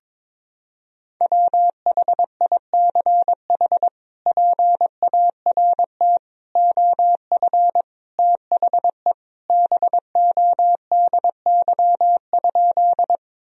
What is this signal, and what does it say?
Morse code, heard as a keyed tone, about 22 wpm, 700 Hz. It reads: WHICH PART OF THE BODY?
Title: Sentences from Top 1000 Words - Review Part 2 - 22wpm